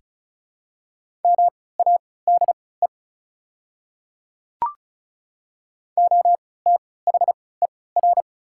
Morse code, heard as a keyed tone, about 36 wpm, spaced as for 20 wpm, 700 Hz. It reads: MADE OTHER
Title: 3x Character Spacing - Top 200 Words - 35wpm